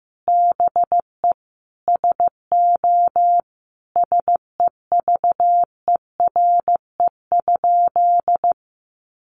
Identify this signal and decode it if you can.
Morse code, heard as a keyed tone, about 15 wpm, 700 Hz. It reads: BE SO SEVERE?
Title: Sentences from Top 1400 Words 15wpm